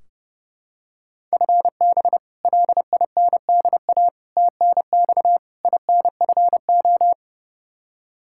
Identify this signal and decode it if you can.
Morse code, heard as a keyed tone, about 32 wpm, 700 Hz. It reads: FB LINDA TNX INFO
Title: Practice QSO - Exchange 3 - More Info - 30wpm